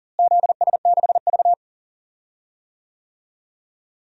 Morse code, heard as a keyed tone, about 40 wpm, 700 Hz. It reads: ZS6V
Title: DXCC-100 Call Signs - 4 Characters - Repeated 2x - 40wpm